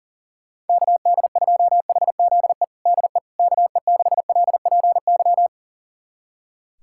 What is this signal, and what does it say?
Morse code, heard as a keyed tone, about 40 wpm, 700 Hz. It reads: KD2HZE DE KE6LPY
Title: USA Callsign DE Callsign - 6 Characters - 40wpm